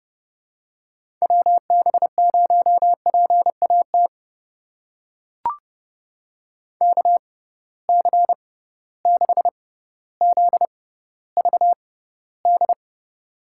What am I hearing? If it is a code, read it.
Morse code, heard as a keyed tone, about 32 wpm, spaced as for 16 wpm, 700 Hz. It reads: WB0PAT KC6ZVD
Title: USA Call Signs - 6 Characters - 6x Character Spacing - 30wpm